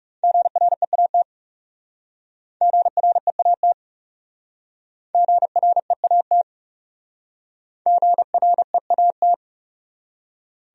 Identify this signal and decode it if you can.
Morse code, heard as a keyed tone, about 38 wpm, 700 Hz. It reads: GREAT GREAT GREAT GREAT